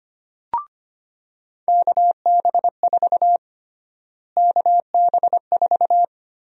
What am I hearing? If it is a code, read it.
Morse code, heard as a keyed tone, about 25 wpm, 700 Hz. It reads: KB4 KB4
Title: USA Call Sign Prefixes - Repeated 2x - 25wpm